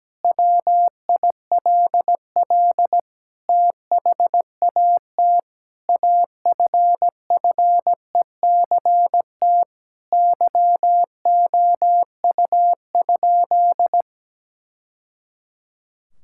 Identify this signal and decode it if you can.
Morse code, heard as a keyed tone, about 17 wpm, 700 Hz. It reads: WILL THAT AFFECT YOU?
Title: Sentences from Top 1000 Words - Review Part 6 - 17wpm